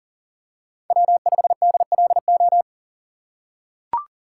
Morse code, heard as a keyed tone, about 39 wpm, 700 Hz. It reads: W5DLO
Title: USA Call Signs - 6x Character Spacing - 40wpm